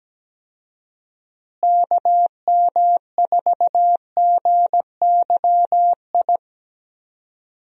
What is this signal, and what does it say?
Morse code, heard as a keyed tone, about 17 wpm, 700 Hz. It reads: KM4GYI
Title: USA Call Signs - 6 Characters - Repeated 2x - 17wpm